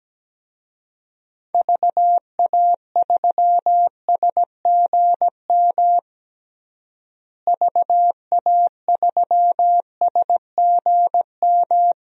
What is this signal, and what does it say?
Morse code, heard as a keyed tone, about 17 wpm, 700 Hz. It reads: VA3SGM VA3SGM
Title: DXCC-100 Call Signs - 6 Characters - Repeated 2x - 17wpm